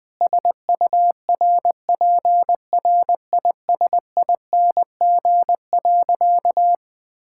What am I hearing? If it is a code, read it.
Morse code, heard as a keyed tone, about 21 wpm, 700 Hz. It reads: SURPRISING.
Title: Sentences with 4-Word Phrases - Spoken Beginning - 20wpm